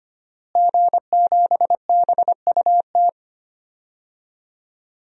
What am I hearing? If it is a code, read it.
Morse code, heard as a keyed tone, about 25 wpm, 700 Hz. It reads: G7BUT
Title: DXCC-100 Call Signs - 5 Characters - Repeated 3x - 25wpm